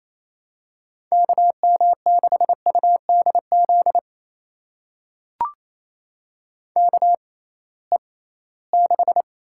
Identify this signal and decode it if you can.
Morse code, heard as a keyed tone, about 28 wpm, spaced as for 18 wpm, 700 Hz. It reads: KM6UDZ KE6
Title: USA Call Signs - 6x Character Spacing - 28wpm